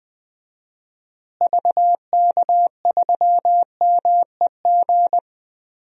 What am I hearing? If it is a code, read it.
Morse code, heard as a keyed tone, about 19 wpm, 700 Hz. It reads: VK3MEG
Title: DXCC-100 Call Signs - 6 Characters - Repeated 3x - 20wpm